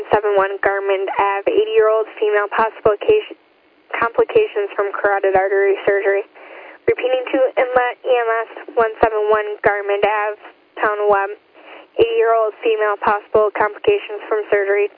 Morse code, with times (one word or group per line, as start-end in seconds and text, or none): none